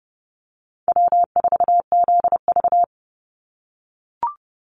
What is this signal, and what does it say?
Morse code, heard as a keyed tone, about 30 wpm, 700 Hz. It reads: W4ZV
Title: NA SOTA Chaser Callsigns 30wpm